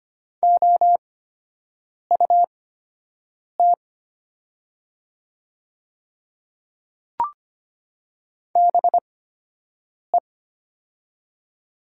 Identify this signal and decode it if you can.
Morse code, heard as a keyed tone, about 25 wpm, spaced as for 7 wpm, 700 Hz. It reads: OUT BE
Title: 8x Character Spacing - Top 100 Words - 25wpm